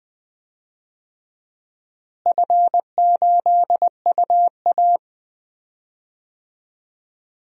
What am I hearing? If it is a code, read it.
Morse code, heard as a keyed tone, about 20 wpm, 700 Hz. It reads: F8UA
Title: DXCC-100 Call Signs - 4 Characters - Repeated 3x - 20wpm